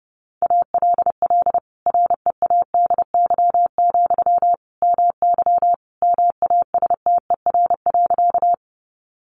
Morse code, heard as a keyed tone, about 30 wpm, 700 Hz. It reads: ALL READY, MY MASTER.
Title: Sentences from Top 1300 Words 30wpm